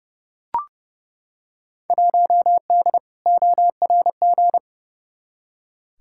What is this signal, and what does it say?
Morse code, heard as a keyed tone, about 30 wpm, 700 Hz. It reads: 1D ORG
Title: Field Day Exchanges - 30wpm